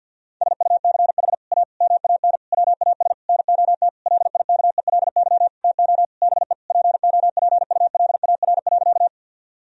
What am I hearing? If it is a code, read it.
Morse code, heard as a keyed tone, about 50 wpm, 700 Hz. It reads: SUCH A MAN WAS NOT LIKELY TO BE POPULAR.
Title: Sentences from Top 1000 Words - Review Part 4 - 50wpm